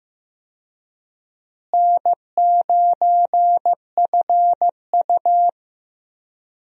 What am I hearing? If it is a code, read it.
Morse code, heard as a keyed tone, about 15 wpm, 700 Hz. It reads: N9FU
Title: USA Call Signs - 4 Characters - Repeated 3x - 15wpm